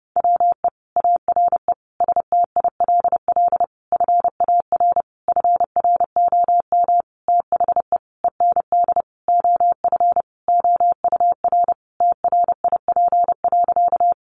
Morse code, heard as a keyed tone, about 30 wpm, 700 Hz. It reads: WE ARE STILL FAR FROM THE END OF OUR TRIP.
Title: Sentences from Top 1300 Words 30wpm